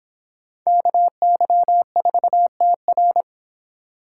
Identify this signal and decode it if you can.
Morse code, heard as a keyed tone, about 24 wpm, 700 Hz. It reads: KY4TR